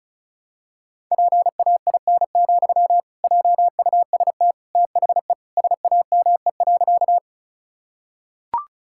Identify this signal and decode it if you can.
Morse code, heard as a keyed tone, about 34 wpm, 700 Hz. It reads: PAIN, JUST THE SAME.
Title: Sentences from Top 1600 Words 35wpm